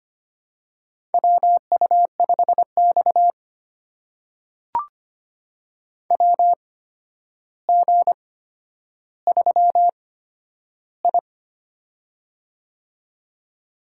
Morse code, heard as a keyed tone, about 25 wpm, spaced as for 11 wpm, 700 Hz. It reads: WU5X WG3I
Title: USA Call Signs - 4 Characters - 8x Character Spacing - 25wpm